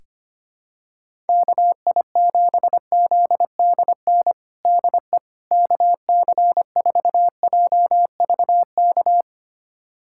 Morse code, heard as a keyed tone, about 25 wpm, 700 Hz. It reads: KI7ZDN DE KC4JVK
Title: USA Callsign DE Callsign - 6 Characters - 25wpm